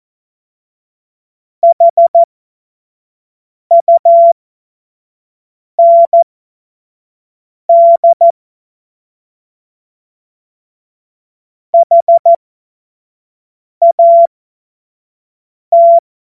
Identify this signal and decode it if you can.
Morse code, heard as a keyed tone, about 14 wpm, spaced as for 5 wpm, 680 Hz. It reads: HUND HAT
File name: Morsekurs / Witze / Witz 19. Oktober 2021, 25 F70